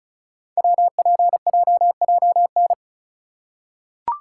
Morse code, heard as a keyed tone, about 36 wpm, 700 Hz. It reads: WPJJN
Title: Code Groups - 5 Letters - 35wpm